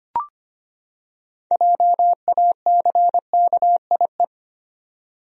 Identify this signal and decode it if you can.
Morse code, heard as a keyed tone, about 25 wpm, 700 Hz. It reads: JACKIE
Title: USA Common Names - 25wpm